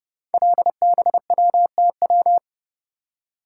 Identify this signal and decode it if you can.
Morse code, heard as a keyed tone, about 30 wpm, 700 Hz. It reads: LBWTW
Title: Code Groups - 5 Letter:Numbers - 30wpm